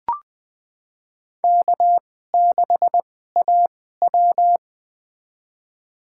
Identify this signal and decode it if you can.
Morse code, heard as a keyed tone, about 20 wpm, spaced as for 15 wpm, 700 Hz. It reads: K6AW
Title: USA Call Signs - 4 Characters - 2x Character Spacing - 20wpm